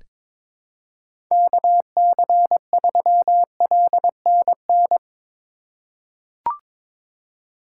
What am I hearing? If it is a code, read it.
Morse code, heard as a keyed tone, about 22 wpm, 700 Hz. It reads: KC3LNN
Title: USA Call Signs - 6 Characters - 4x Character Spacing - 22wpm